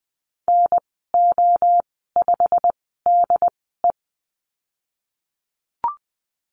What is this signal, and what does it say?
Morse code, heard as a keyed tone, about 20 wpm, spaced as for 15 wpm, 700 Hz. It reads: NO5DE
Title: 088 Lesson 15g - Number 5 - Call Signs 20wpm